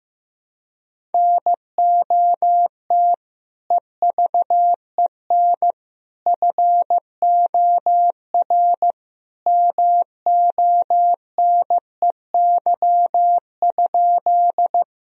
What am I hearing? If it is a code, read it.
Morse code, heard as a keyed tone, about 15 wpm, 700 Hz. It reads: NOT EVEN FOR MONEY?
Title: Sentences from Top 1000 Words - Review Part 1 - 15wpm